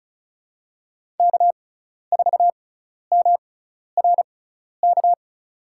Morse code, heard as a keyed tone, about 36 wpm, spaced as for 12 wpm, 700 Hz. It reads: K4MRK
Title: USA Call Signs - 6x Character Spacing - 35wpm